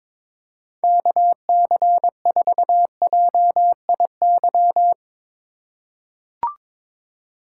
Morse code, heard as a keyed tone, about 22 wpm, 700 Hz. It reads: KC4JIY
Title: USA Call Signs - 4x Character Spacing - 22wpm